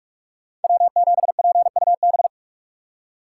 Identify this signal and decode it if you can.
Morse code, heard as a keyed tone, about 41 wpm, 700 Hz. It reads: W7PUB